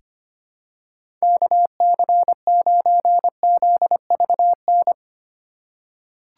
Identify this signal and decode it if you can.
Morse code, heard as a keyed tone, about 25 wpm, 700 Hz. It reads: KC9ZVN
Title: USA Call Signs - 6 Characters - 2x Character Spacing - 25wpm